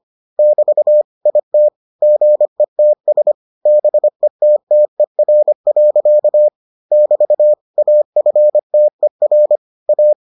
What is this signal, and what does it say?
Morse code, heard as a keyed tone, about 25 wpm, 600 Hz. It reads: <BT> IT GETS BETTER. <BT> AFTER A